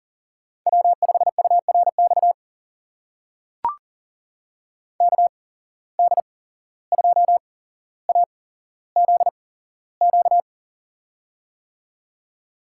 Morse code, heard as a keyed tone, about 40 wpm, spaced as for 16 wpm, 700 Hz. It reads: W5URX KD2AZQ
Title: USA Call Signs - 8x Character Spacing - 40wpm